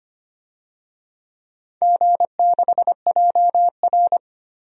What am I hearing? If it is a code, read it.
Morse code, heard as a keyed tone, about 25 wpm, 700 Hz. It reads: G6JR
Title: DXCC-100 Call Signs - 4 Characters - 25wpm